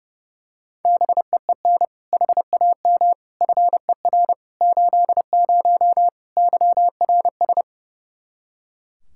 Morse code, heard as a keyed tone, about 30 wpm, 700 Hz. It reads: BEEN HAM FER 80 YRS
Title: K1USN SST QSO Practice 30wpm